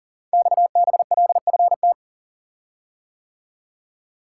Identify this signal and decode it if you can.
Morse code, heard as a keyed tone, about 40 wpm, 700 Hz. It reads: XBLFT
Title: Code Groups - 5 Letters - 40wpm